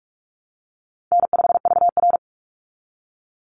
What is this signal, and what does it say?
Morse code, heard as a keyed tone, about 44 wpm, 700 Hz. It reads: N5VR
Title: Easy Call Signs Repeated 3x - 45wpm